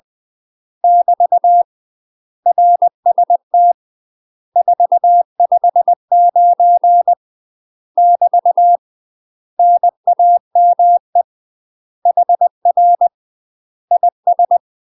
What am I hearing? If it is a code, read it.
Morse code, heard as a keyed tone, about 20 wpm, 700 Hz. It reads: <BT> RST 459 <BT> NAME HR IS
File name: UK MASTER QSOs TRAINING / 20WPM / UK MASTER QSOs 0052